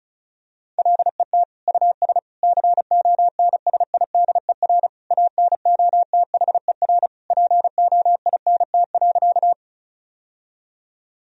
Morse code, heard as a keyed tone, about 36 wpm, 700 Hz. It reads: LET US CONSIDER ANOTHER POINT.
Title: Sentences from Top 1000 Words - Review Part 4 - 35wpm